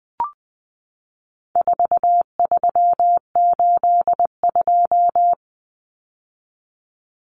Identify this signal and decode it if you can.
Morse code, heard as a keyed tone, about 20 wpm, 700 Hz. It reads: 4382
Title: Common Cut Numbers - 4-Digit 20wpm